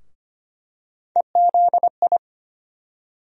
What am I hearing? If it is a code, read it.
Morse code, heard as a keyed tone, about 25 wpm, 700 Hz. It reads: EZI
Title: USA Call Sign Postfixes - 3 Letters 25wpm